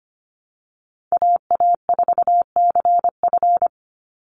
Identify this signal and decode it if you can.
Morse code, heard as a keyed tone, about 25 wpm, 700 Hz. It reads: AA4CF
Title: Call Signs - 25wpm